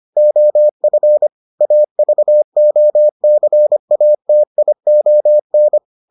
Morse code, heard as a keyed tone, about 25 wpm, 600 Hz. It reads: OF AVOCATION